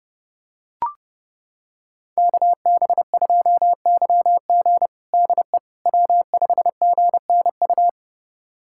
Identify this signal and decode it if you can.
Morse code, heard as a keyed tone, about 30 wpm, 700 Hz. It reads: KB2YG DE W5GNU
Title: USA Callsign DE Callsign - 5 Characters - 30wpm